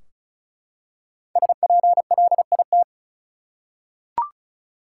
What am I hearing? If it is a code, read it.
Morse code, heard as a keyed tone, about 34 wpm, 700 Hz. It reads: SPLIT
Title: Five Letter Words - 35wpm